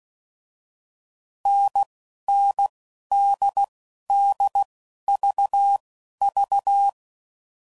Morse code, heard as a keyed tone, about 16 wpm, spaced as for 11 wpm, 800 Hz. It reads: NNDDVV
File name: Combinations / Difficult Combos / NDB NDB6